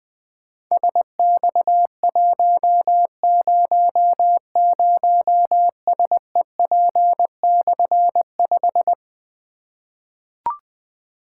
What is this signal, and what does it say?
Morse code, heard as a keyed tone, about 20 wpm, 700 Hz. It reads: SX100SEP/5
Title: Difficult Call Signs 20wpm